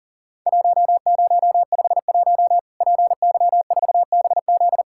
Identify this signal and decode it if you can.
Morse code, heard as a keyed tone, about 40 wpm, 700 Hz. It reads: 1051 PY4BZ
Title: Worked All Europe DX Contest - QTC Exchange - 40wpm